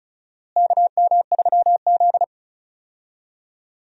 Morse code, heard as a keyed tone, about 34 wpm, 700 Hz. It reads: KM3Z
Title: USA Call Signs - 4 Characters - Repeated 2x - 35wpm